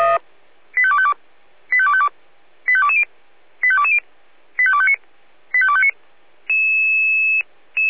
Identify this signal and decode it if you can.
Five-tone selective call (ZVEI), heard as quick runs of five tones, tones of about 70 ms. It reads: 85252 85252 85220 85220 85259 85259